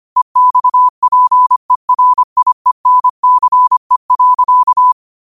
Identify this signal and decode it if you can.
Morse code, heard as a keyed tone, about 25 wpm, 1000 Hz. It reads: EXPERIENCE.